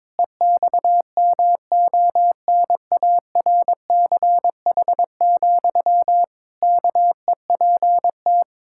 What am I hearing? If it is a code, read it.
Morse code, heard as a keyed tone, about 22 wpm, 700 Hz. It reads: EXMONARCH, KEPT
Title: Pangram Challenge v2 - 22wpm